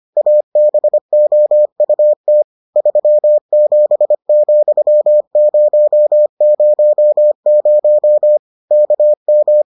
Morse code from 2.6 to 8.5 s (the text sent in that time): 37,000